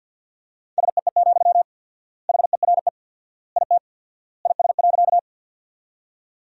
Morse code, heard as a keyed tone, about 50 wpm, 700 Hz. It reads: SEE, HERE IT IS.
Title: 4x Word-Spacing - Sentences from Top 200 Words - 50wpm